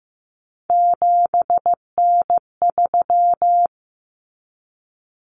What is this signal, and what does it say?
Morse code, heard as a keyed tone, about 15 wpm, 700 Hz. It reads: 7N3
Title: Common Cut Numbers - 3-Digit 15wpm